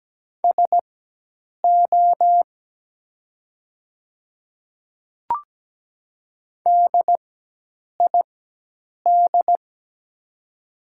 Morse code, heard as a keyed tone, about 17 wpm, spaced as for 8 wpm, 700 Hz. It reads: SO DID
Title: 4x Character Spacing - Top 300 Words - 17wpm